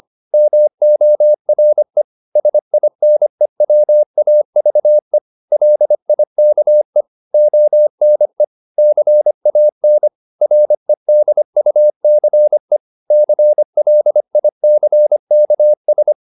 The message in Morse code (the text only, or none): MORE SINE WAVE LIKE ONE CAN REDUCE CLICKS